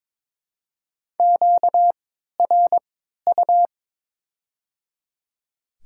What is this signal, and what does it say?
Morse code, heard as a keyed tone, about 21 wpm, spaced as for 12 wpm, 700 Hz. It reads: QRU
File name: Top 90 QSO Elements - 3x Character Spacing - 22wpm